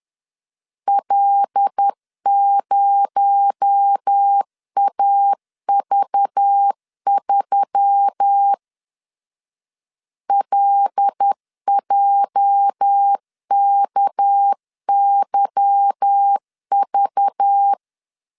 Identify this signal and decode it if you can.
Morse code, heard as a keyed tone, about 10 wpm, 800 Hz. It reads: L0AV3 LJKYV
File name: LEZIONI / 8 YJ45 / 8 YJ 10wpm 2651